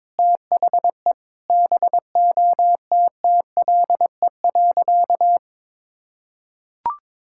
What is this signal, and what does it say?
Morse code, heard as a keyed tone, about 22 wpm, 700 Hz. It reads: THE BOTTLE.
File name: practice / 22wpm / Sentences with 4-Word Phrases - 22wpm